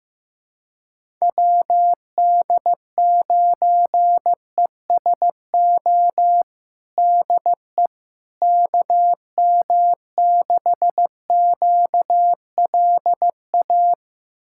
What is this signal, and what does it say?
Morse code, heard as a keyed tone, about 15 wpm, 700 Hz. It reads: WD9ESO DE KM6QLA
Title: USA Callsign DE Callsign 15wpm